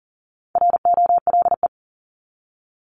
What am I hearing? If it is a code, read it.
Morse code, heard as a keyed tone, about 40 wpm, 700 Hz. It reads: ROLE
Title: Top 751-1000 Words 40wpm